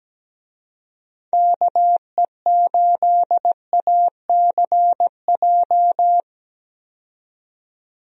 Morse code, heard as a keyed tone, about 17 wpm, 700 Hz. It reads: KE8ACJ
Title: USA Call Signs - 6 Characters - 17wpm